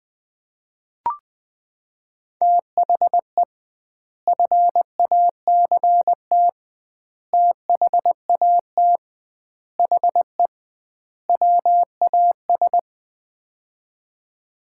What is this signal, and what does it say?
Morse code, heard as a keyed tone, about 20 wpm, 700 Hz. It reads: THE FACT THAT HE WAS
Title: 2x Word-Spacing - Sets of 5 Words - 20wpm